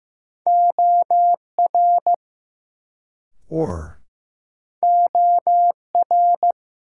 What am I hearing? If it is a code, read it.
Morse code, heard as a keyed tone, about 15 wpm, 700 Hz. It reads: OR OR
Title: Sets of 2 Words - New Word Drill - 15wpm